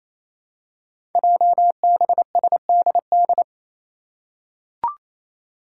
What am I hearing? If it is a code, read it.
Morse code, heard as a keyed tone, about 28 wpm, 700 Hz. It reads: JBSDD
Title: Five Challenging Letter-Number Combinations 28wpm